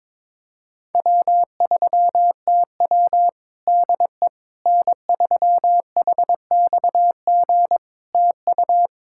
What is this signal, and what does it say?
Morse code, heard as a keyed tone, about 22 wpm, 700 Hz. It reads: W3TW DE N3HXG TU
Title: Practice QSO - Exchange 4 - Ending - 22wpm